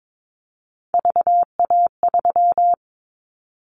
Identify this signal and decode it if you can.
Morse code, heard as a keyed tone, about 22 wpm, 700 Hz. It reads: VA3
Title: Callsign Prefixes for Top 25 DXCC - 22wpm